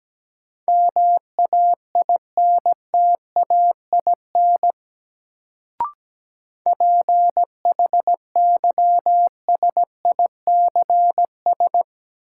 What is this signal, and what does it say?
Morse code, heard as a keyed tone, about 17 wpm, 700 Hz. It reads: MAINTAIN PHYSICS
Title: Sentences from Top 1000 Words - Review Part 5 - New Word Drill - 17wpm